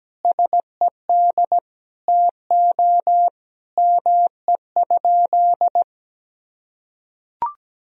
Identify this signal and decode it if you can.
Morse code, heard as a keyed tone, about 17 wpm, 700 Hz. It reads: SED TO ME?
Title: Sentences from Top 200 Words and Abbreviations 17wpm